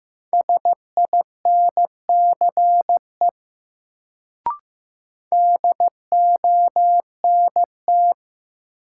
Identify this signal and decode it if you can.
Morse code, heard as a keyed tone, about 16 wpm, 700 Hz. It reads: SINCE DONT
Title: Sentences from Top 300 Words - New Word Drill - 15wpm